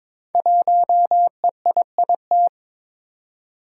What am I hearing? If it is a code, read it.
Morse code, heard as a keyed tone, about 22 wpm, 700 Hz. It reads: 1EIIT
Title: Five Challenging Letter-Number Combinations 22wpm